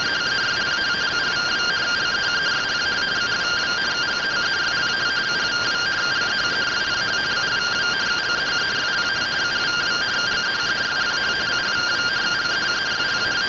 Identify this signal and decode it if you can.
Two-tone teleprinter signal, about 170 Hz shift.
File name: findings / AXIAL fr1 3mm.gen-29 rtty rx